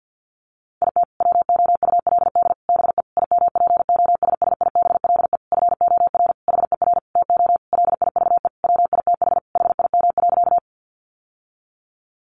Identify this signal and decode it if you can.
Morse code, heard as a keyed tone, about 50 wpm, 700 Hz. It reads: IT WOULD BE IMPOSSIBLE FOR HER TO LIVE WITH HIM.
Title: Sentences from Top 1100 Words 50wpm